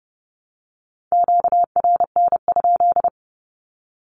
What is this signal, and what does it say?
Morse code, heard as a keyed tone, about 30 wpm, 700 Hz. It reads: QRN?
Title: Q-Codes Repeated 3x 30wpm